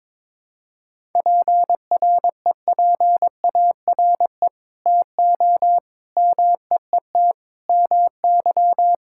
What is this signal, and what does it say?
Morse code, heard as a keyed tone, about 22 wpm, 700 Hz. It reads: PREPARE TO MEET MY
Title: Sentences from Top 1000 Words - Review Part 6 - 22wpm